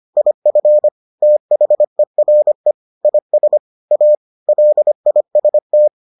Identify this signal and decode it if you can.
Morse code, heard as a keyed tone, about 25 wpm, 600 Hz. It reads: IF THERE IS A LIST